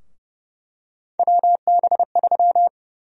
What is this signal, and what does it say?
Morse code, heard as a keyed tone, about 29 wpm, 700 Hz. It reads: WB3
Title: USA Call Sign Prefixes - 30wpm